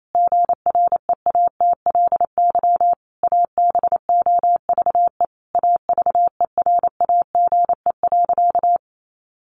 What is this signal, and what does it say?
Morse code, heard as a keyed tone, about 28 wpm, 700 Hz. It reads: GREATLY ABOVE AVERAGE.
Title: Sentences from Top 1400 Words 28wpm